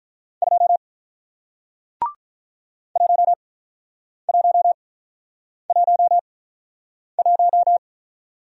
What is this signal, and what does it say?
Morse code, heard as a keyed tone, about 48 wpm, spaced as for 9 wpm, 700 Hz. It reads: 2 1111